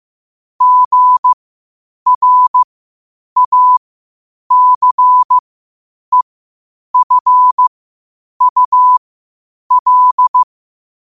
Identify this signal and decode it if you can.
Morse code, heard as a keyed tone, about 15 wpm, spaced as for 8 wpm, 1000 Hz. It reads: GRACEFUL
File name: qotdfiles / 8wpm